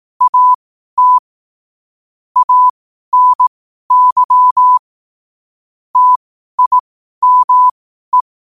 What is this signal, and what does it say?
Morse code, heard as a keyed tone, about 18 wpm, spaced as for 12 wpm, 1000 Hz. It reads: AT ANY TIME
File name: qotdfiles / 12wpm